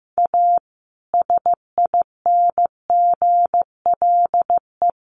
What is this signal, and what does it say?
Morse code, heard as a keyed tone, about 15 wpm, 700 Hz. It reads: A SINGLE
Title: Sentences from Top 1100 Words 15wpm